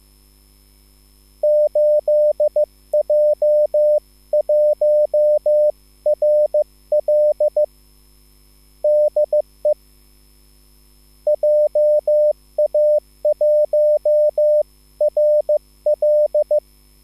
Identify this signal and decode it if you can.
Morse code, heard as a keyed tone, about 15 wpm, 600 Hz. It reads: 8J1RL DE JA1RL